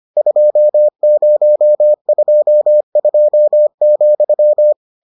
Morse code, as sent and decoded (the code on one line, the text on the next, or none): ..--- ----- ..--- ..--- --..--
2022,